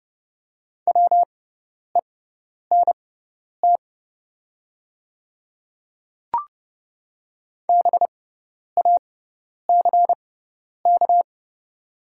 Morse code, heard as a keyed tone, about 30 wpm, spaced as for 10 wpm, 700 Hz. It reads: WENT BACK
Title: 6x Character Spacing - Top 300 Words - 30wpm